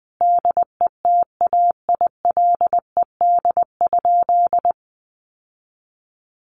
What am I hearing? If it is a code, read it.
Morse code, heard as a keyed tone, about 19 wpm, 700 Hz. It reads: DETAILED?
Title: Sentences from Top 1500 Words 20wpm